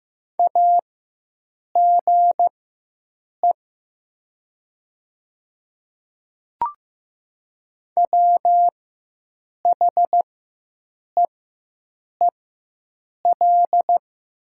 Morse code, heard as a keyed tone, about 15 wpm, spaced as for 7 wpm, 700 Hz. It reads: AGE WHEEL
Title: 4x Character Spacing - Top 500 Words - 15wpm